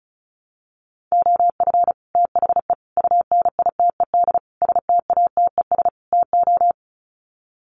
Morse code, heard as a keyed tone, about 34 wpm, 700 Hz. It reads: OF THE UNITED STATES TO
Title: Sets of 5 Words Encore 35wpm